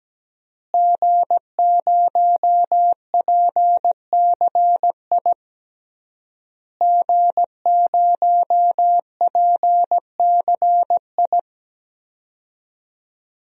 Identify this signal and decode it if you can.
Morse code, heard as a keyed tone, about 17 wpm, 700 Hz. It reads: G0PCI G0PCI
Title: DXCC-100 Call Signs - 5 Characters - Repeated 3x - 17wpm